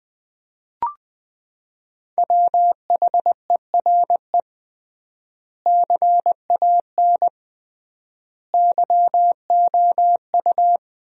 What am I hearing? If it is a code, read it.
Morse code, heard as a keyed tone, about 20 wpm, 700 Hz. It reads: WHERE CAN YOU
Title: 3x Word-Spacing - Sentences from Top 500 Words - 20wpm